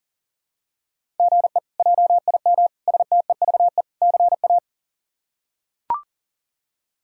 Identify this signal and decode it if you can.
Morse code, heard as a keyed tone, about 37 wpm, 700 Hz. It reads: GE JIM STEVE CA
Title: K1USN SST QSO Practice - 40wpm